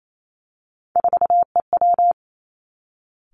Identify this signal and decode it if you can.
Morse code, heard as a keyed tone, about 28 wpm, 700 Hz. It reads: 4EW
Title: Three Challenging Letter-Number Combinations 28wpm